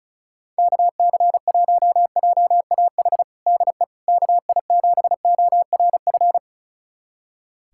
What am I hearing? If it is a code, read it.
Morse code, heard as a keyed tone, about 36 wpm, 700 Hz. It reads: KC1JAH DE KI7ORF
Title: USA Callsign DE Callsign 35wpm